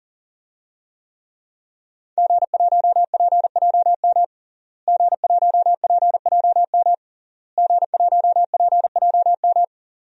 Morse code, heard as a keyed tone, about 40 wpm, 700 Hz. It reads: G1PJM G1PJM G1PJM
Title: DXCC-100 Call Signs - 5 Characters - Repeated 3x - 40wpm